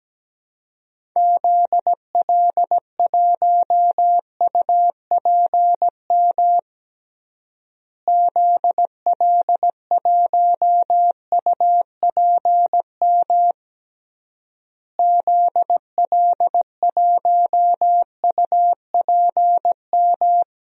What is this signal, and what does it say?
Morse code, heard as a keyed tone, about 17 wpm, 700 Hz. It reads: ZL1UPM ZL1UPM ZL1UPM
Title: DXCC-100 Call Signs - 6 Characters - Repeated 3x - 17wpm